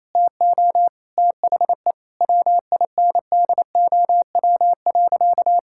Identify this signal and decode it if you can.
Morse code, heard as a keyed tone, about 28 wpm, 700 Hz. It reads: TO THE WINDOW.
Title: Sentences from Top 1600 Words 28wpm